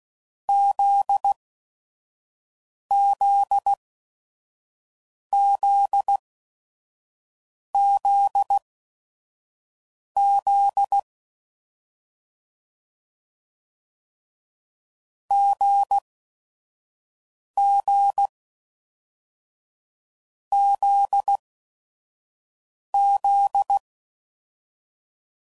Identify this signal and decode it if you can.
Morse code, heard as a keyed tone, about 16 wpm, spaced as for 5 wpm, 800 Hz. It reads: ZZZZZ GGZZ